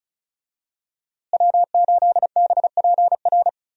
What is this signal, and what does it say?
Morse code, heard as a keyed tone, about 36 wpm, 700 Hz. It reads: W8BPR
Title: USA Call Signs - 5 Characters - Repeated 3x - 35wpm